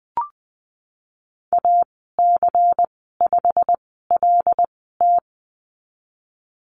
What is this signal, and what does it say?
Morse code, heard as a keyed tone, about 20 wpm, spaced as for 15 wpm, 700 Hz. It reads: AC5LT
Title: 128 Lesson 21g - Number 6 - Call Signs 20wpm